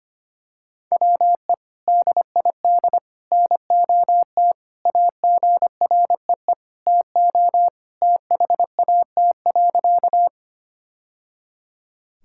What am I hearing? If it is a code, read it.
Morse code, heard as a keyed tone, about 25 wpm, 700 Hz. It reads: WE DID NOT AGREE TO THAT.
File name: practice / 25wpm / Sentences with 3-Word Phrases - 25wpm